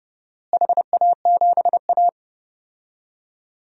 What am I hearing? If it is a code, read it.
Morse code, heard as a keyed tone, about 30 wpm, 700 Hz. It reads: HA7A
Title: DXCC-100 Call Signs - 4 Characters - Repeated 3x - 30wpm